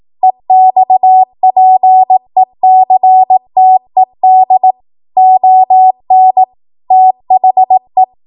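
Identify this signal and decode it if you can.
Morse code, heard as a keyed tone, about 18 wpm, 750 Hz. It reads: EXPECTED ON THE
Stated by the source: 18 WPM Morse code practice from a national amateur radio association's code archive.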